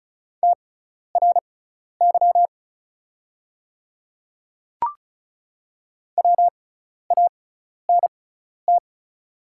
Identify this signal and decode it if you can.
Morse code, heard as a keyed tone, about 36 wpm, spaced as for 12 wpm, 700 Hz. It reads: TRY WANT
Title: 6x Character Spacing - Top 200 Words - 35wpm